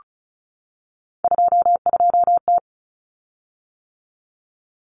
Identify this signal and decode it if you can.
Morse code, heard as a keyed tone, about 34 wpm, 700 Hz. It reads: 22T
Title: Common Cut Numbers - 3-Digit 35wpm